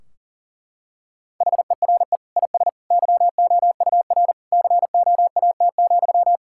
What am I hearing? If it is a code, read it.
Morse code, heard as a keyed tone, about 40 wpm, 700 Hz. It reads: HERE IS YOUR COAT,
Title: Sentences from Top 1600 Words 40wpm